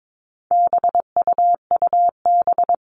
Morse code, heard as a keyed tone, about 22 wpm, 700 Hz. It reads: BUUB
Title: All Cut Numbers - 4-Digit 22wpm